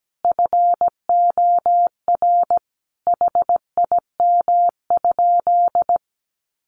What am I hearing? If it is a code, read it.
Morse code, heard as a keyed tone, about 17 wpm, 700 Hz. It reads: FOR HIM?
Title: Sentences from Top 400 Words 17wpm